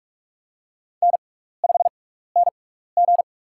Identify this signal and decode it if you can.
Morse code, heard as a keyed tone, about 45 wpm, spaced as for 15 wpm, 700 Hz. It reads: N5NG
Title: USA Call Signs - 4 Characters - 6x Character Spacing - 45wpm